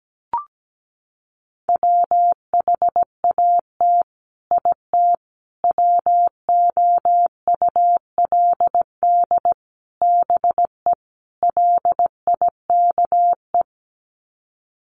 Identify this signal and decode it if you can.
Morse code, heard as a keyed tone, about 17 wpm, 700 Hz. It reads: WHAT IT WOULD BE LIKE
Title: Sets of 5 Words Encore 17wpm